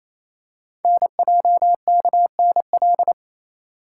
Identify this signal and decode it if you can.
Morse code, heard as a keyed tone, about 29 wpm, 700 Hz. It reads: NJKNL
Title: Code Groups - 5 Letter:Numbers - 28wpm